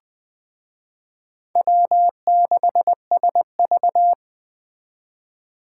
Morse code, heard as a keyed tone, about 21 wpm, 700 Hz. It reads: W6SV
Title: Speed Racing - USA 1x2 Call Signs 17wpm